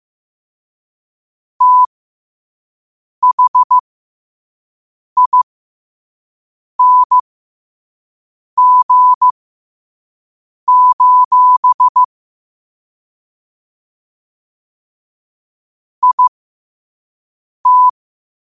Morse code, heard as a keyed tone, about 15 wpm, spaced as for 5 wpm, 1000 Hz. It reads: THING: IT
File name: qotdfiles / 5wpm